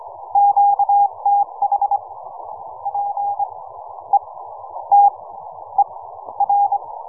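Morse code, heard as a keyed tone, about 24 wpm, spaced as for 13 wpm, 810 Hz. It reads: QTH PETER
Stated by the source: QRP Morse code signal.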